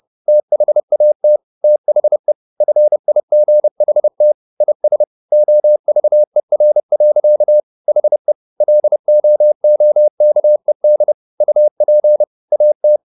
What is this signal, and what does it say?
Morse code, heard as a keyed tone, about 30 wpm, 600 Hz. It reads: THAT THE FIGHT IS OVER. HE LOOKED UP AT